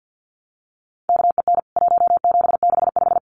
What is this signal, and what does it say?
Morse code, heard as a keyed tone, about 48 wpm, 700 Hz. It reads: KEN 1765
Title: CWT Exchange 50wpm